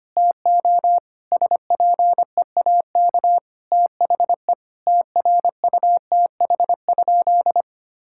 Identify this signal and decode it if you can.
Morse code, heard as a keyed tone, about 25 wpm, 700 Hz. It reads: TO SPEAK THE TRUTH?
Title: Sentences from Top 1600 Words 25wpm